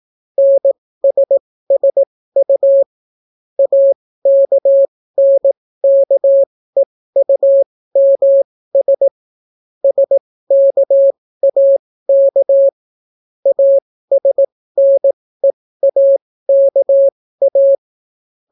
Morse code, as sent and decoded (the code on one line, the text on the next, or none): -. ... ... ..- / .- -.- -. -.- . ..- -- ... / ... -.- .- -.- / .- ... -. .- -.- .-
NSSU AKNKEUMS SKAK ASNEAKA